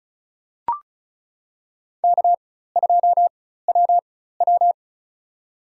Morse code, heard as a keyed tone, about 36 wpm, spaced as for 16 wpm, 700 Hz. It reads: K2WW
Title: USA Call Signs - 4x Character Spacing - 35wpm